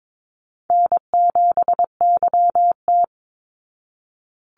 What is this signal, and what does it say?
Morse code, heard as a keyed tone, about 22 wpm, 700 Hz. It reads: N7YT
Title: Call Signs Repeated 3x - 22wpm